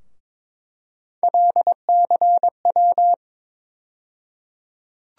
Morse code, heard as a keyed tone, about 22 wpm, 700 Hz. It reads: LCW
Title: USA Call Sign Postfixes - 3 Letters 22wpm